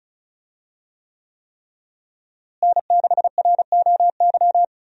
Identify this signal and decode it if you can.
Morse code, heard as a keyed tone, about 34 wpm, 700 Hz. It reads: N6ROY
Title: USA Call Signs - 5 Characters - Repeated 3x - 35wpm